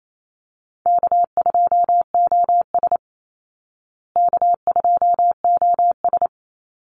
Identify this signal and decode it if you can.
Morse code, heard as a keyed tone, about 28 wpm, 700 Hz. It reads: K2OS K2OS
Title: Call Signs Repeated 2x - 28wpm